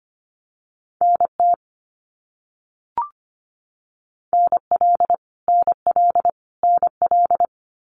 Morse code, heard as a keyed tone, about 25 wpm, 700 Hz. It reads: NT NL NL NL
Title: Canadian Provinces and Territories - Repeated 3x - 25wpm